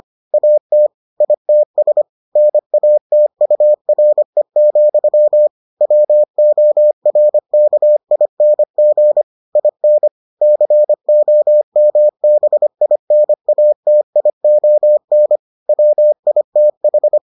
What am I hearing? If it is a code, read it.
Morse code, heard as a keyed tone, about 25 wpm, 600 Hz. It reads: AT ITS NATURE, WORKING IN COMBINATION WITH